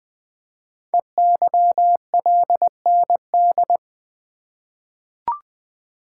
Morse code, heard as a keyed tone, about 20 wpm, 700 Hz. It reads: EYLND
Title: Code Groups - 5 Letters - 20wpm